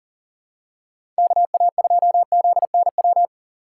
Morse code, heard as a keyed tone, about 40 wpm, 700 Hz. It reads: KA2ZNW
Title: USA Call Signs - 6 Characters - 4x Character Spacing - 40wpm